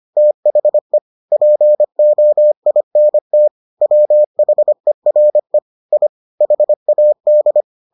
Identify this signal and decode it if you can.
Morse code, heard as a keyed tone, about 25 wpm, 600 Hz. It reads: THE POINT WHERE I HAD